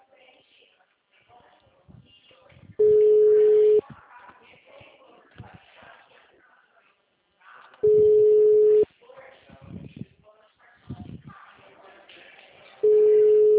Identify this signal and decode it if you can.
Ringback tone (Europe, 425 Hz), with speech.